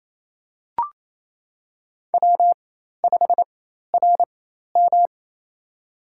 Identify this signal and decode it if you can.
Morse code, heard as a keyed tone, about 27 wpm, spaced as for 13 wpm, 700 Hz. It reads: W5RM
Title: USA Call Signs - 4 Characters - 4x Character Spacing - 28wpm